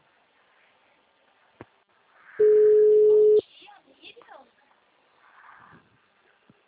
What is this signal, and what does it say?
Ringback tone (Europe, 425 Hz), with speech.